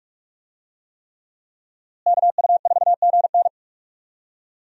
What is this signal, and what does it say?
Morse code, heard as a keyed tone, about 41 wpm, 700 Hz. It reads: KU4GN
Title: USA Call Signs - 5 Characters - Repeated 3x - 45wpm